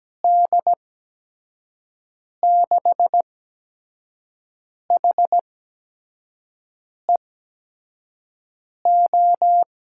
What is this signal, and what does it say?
Morse code, heard as a keyed tone, about 17 wpm, spaced as for 5 wpm, 700 Hz. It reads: D6HEO
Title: USA Call Signs - 6 Characters - 8x Character Spacing - 17wpm